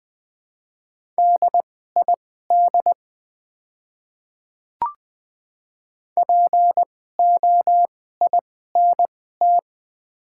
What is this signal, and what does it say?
Morse code, heard as a keyed tone, about 20 wpm, spaced as for 15 wpm, 700 Hz. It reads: DID POINT
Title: 2x Character Spacing - Top 200 Words - 20wpm